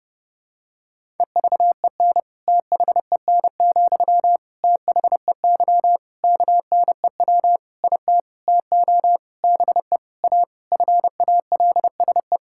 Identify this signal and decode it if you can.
Morse code, heard as a keyed tone, about 30 wpm, 700 Hz. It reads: EVEN THEN, THEY KNEW IT TO BE A FALSE
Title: Sentences from Top 1600 Words 30wpm